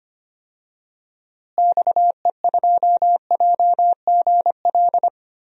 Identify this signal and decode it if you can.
Morse code, heard as a keyed tone, about 25 wpm, 700 Hz. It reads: XE2JGL
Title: DXCC-100 Call Signs - 6 Characters - Repeated 3x - 25wpm